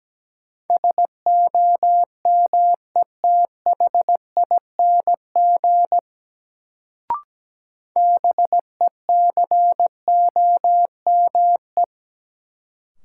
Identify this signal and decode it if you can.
Morse code, heard as a keyed tone, about 17 wpm, 700 Hz. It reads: SOMETHING BECOME
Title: Sentences from Top 600 Words - New Word Drill - 17wpm